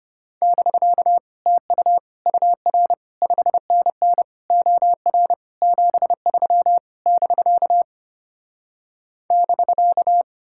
Morse code, heard as a keyed tone, about 33 wpm, 700 Hz. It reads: <BK> TU UR 5NN OR 73 <BK> <BK>